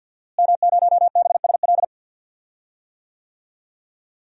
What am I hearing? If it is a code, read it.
Morse code, heard as a keyed tone, about 50 wpm, 700 Hz. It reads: M0BSL
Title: DXCC-100 Call Signs - 5 Characters - Repeated 2x - 50wpm